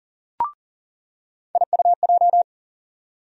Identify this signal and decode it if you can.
Morse code, heard as a keyed tone, about 40 wpm, 700 Hz. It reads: IUJ
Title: USA Call Sign Postfixes - 3 Letters 40wpm